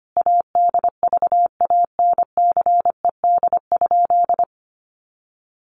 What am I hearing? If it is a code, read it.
Morse code, heard as a keyed tone, about 25 wpm, 700 Hz. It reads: ADVANCED?
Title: Sentences from Top 1200 Words 25wpm